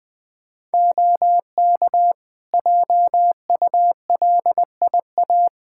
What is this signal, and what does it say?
Morse code, heard as a keyed tone, about 20 wpm, 700 Hz. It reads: OK JULIA
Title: Practice QSO - Exchange 4 - Ending - 20wpm